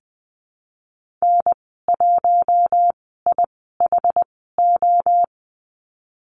Morse code, heard as a keyed tone, about 20 wpm, 700 Hz. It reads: N 1 I H O
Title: 076 Lesson 13g - Letter L - Call Signs 20wpm